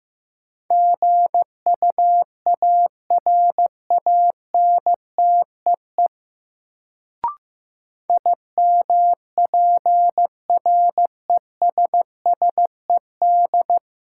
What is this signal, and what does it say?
Morse code, heard as a keyed tone, about 15 wpm, 700 Hz. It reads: GUARANTEE IMPRESSED
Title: Sentences from Top 1600 Words - New Word Drill - 15wpm